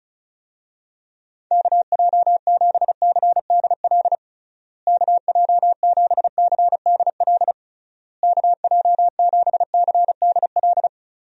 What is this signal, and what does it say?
Morse code, heard as a keyed tone, about 34 wpm, 700 Hz. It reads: KJ7CDL KJ7CDL KJ7CDL
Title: USA Call Signs - 6 Characters - Repeated 3x - 35wpm